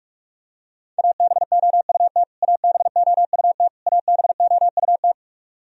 Morse code, heard as a keyed tone, about 46 wpm, 700 Hz. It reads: ABOUT ABOUT ABOUT
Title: Top 100 Words Repeated 45wpm